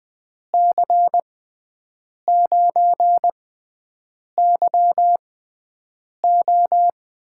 Morse code, heard as a keyed tone, about 21 wpm, spaced as for 7 wpm, 700 Hz. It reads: C9YO
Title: USA Call Signs - 6 Characters - 6x Character Spacing - 20wpm